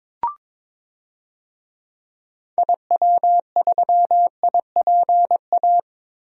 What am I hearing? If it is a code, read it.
Morse code, heard as a keyed tone, about 22 wpm, 700 Hz. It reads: IW3IPA
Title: DXCC-100 Call Signs - 6 Characters - Repeated 3x - 22wpm